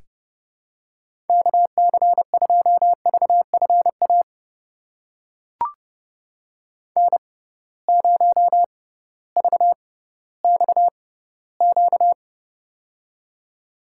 Morse code, heard as a keyed tone, about 32 wpm, spaced as for 17 wpm, 700 Hz. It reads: KC2VFA N0VXQ
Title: USA Call Signs - 6x Character Spacing - 30wpm